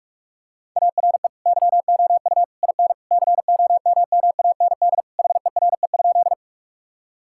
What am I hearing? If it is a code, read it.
Morse code, heard as a keyed tone, about 46 wpm, 700 Hz. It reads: ARE YOU IN COMMAND HERE?
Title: Sentences from Top 1600 Words 45wpm